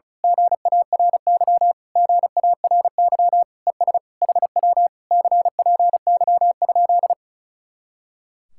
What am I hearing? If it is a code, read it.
Morse code, heard as a keyed tone, about 33 wpm, 700 Hz. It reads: GARY GARY ES HW CPY?
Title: Speed-Racing Common QSO Phrases - 35wpm